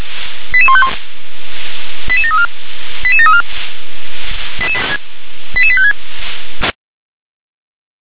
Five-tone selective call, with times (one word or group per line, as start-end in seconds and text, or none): 0.5-0.9 88151
2.1-2.5 88734
3.0-3.4 88734
5.6-5.9 88856